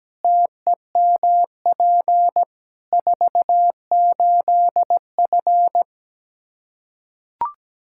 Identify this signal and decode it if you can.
Morse code, heard as a keyed tone, about 17 wpm, 700 Hz. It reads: TEMP 48F
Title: Practice QSO - Exchange 2 - Info - 17wpm